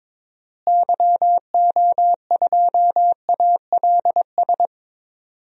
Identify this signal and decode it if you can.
Morse code, heard as a keyed tone, about 21 wpm, 700 Hz. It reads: YO2ALS
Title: DXCC-100 Call Signs - 6 Characters - Repeated 3x - 22wpm